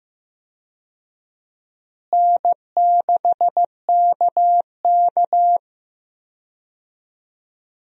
Morse code, heard as a keyed tone, about 15 wpm, 700 Hz. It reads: N6KK
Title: USA Call Signs - 4 Characters - Repeated 2x - 15wpm